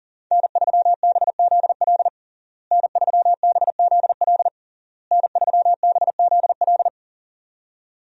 Morse code, heard as a keyed tone, about 40 wpm, 700 Hz. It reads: N3BZL N3BZL N3BZL
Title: USA Call Signs - 5 Characters - Repeated 3x - 40wpm